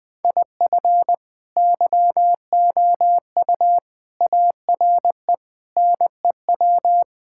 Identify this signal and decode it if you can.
Morse code, heard as a keyed tone, about 21 wpm, 700 Hz. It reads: IF YOU ARE NEW
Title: Sentences with 8-Word Phrases - Spoken Beginning - 20wpm